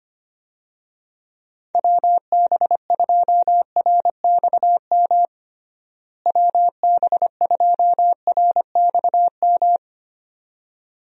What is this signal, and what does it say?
Morse code, heard as a keyed tone, about 25 wpm, 700 Hz. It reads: WB2RXM WB2RXM
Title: USA Call Signs - 6 Characters - Repeated 2x - 25wpm